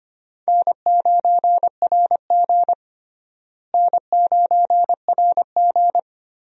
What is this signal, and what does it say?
Morse code, heard as a keyed tone, about 25 wpm, 700 Hz. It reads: N9RG N9RG
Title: USA Call Signs - 4 Characters - Repeated 3x - 25wpm